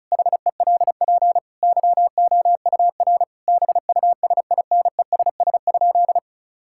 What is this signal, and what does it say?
Morse code, heard as a keyed tone, about 36 wpm, 700 Hz. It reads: HELP YOUR BUSINESS?
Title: Sentences from Top 1000 Words - Review Part 5 - 35wpm